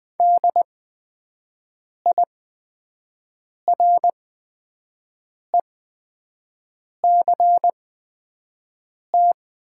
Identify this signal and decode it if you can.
Morse code, heard as a keyed tone, about 20 wpm, spaced as for 5 wpm, 700 Hz. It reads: DIRECT
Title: 8x Character Spacing - Top 500 Words - 20wpm